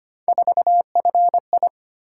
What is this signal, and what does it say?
Morse code, heard as a keyed tone, about 25 wpm, 700 Hz. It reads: 4FI
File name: Three Letter-Number Combinations - 25wpm